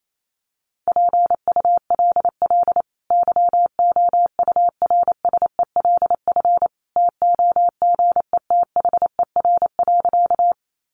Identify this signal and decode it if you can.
Morse code, heard as a keyed tone, about 28 wpm, 700 Hz. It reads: PULL YOURSELF TOGETHER.
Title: Sentences from Top 600 Words 28wpm